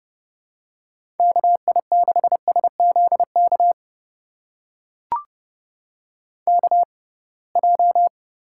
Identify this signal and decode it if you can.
Morse code, heard as a keyed tone, about 30 wpm, spaced as for 23 wpm, 700 Hz. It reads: KI6SZK KJ
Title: USA Call Signs - 6 Characters - 6x Character Spacing - 30wpm